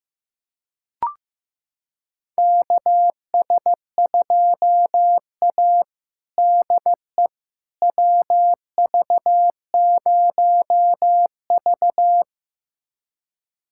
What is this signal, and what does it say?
Morse code, heard as a keyed tone, about 15 wpm, 700 Hz. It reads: KS2A DE WV0V
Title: USA Callsign DE Callsign - 4 Characters - 15wpm